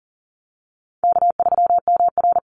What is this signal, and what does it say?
Morse code, heard as a keyed tone, about 40 wpm, 700 Hz. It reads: K3MR
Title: Easy Call Signs Repeated 3x - 40wpm